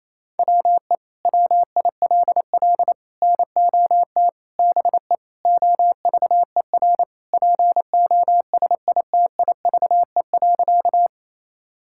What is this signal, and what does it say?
Morse code, heard as a keyed tone, about 28 wpm, 700 Hz. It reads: WE WILL NOT BE OVER POSITIVE.
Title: Sentences from Top 1000 Words - Review Part 5 - 28wpm